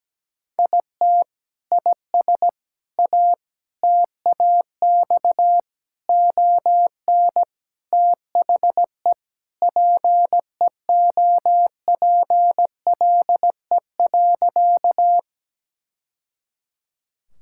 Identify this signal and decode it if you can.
Morse code, heard as a keyed tone, about 17 wpm, 700 Hz. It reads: IT IS A TAX ON THE PEOPLE.
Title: Sentences from Top 1000 Words - Review Part 5 - 17wpm